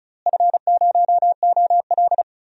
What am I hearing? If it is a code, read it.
Morse code, heard as a keyed tone, about 34 wpm, 700 Hz. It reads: F0OL
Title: DXCC-100 Call Signs - 4 Characters - Repeated 3x - 35wpm